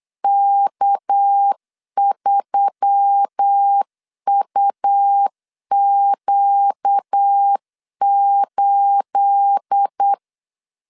Morse code, heard as a keyed tone, about 8 wpm, 800 Hz. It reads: K3UQ8
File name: LEZIONI / 9 28 8wpm 3512